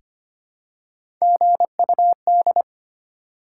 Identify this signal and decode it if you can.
Morse code, heard as a keyed tone, about 25 wpm, 700 Hz. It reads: GUD
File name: Top 20 QSO Elements - 3x Character Spacing - 25wpm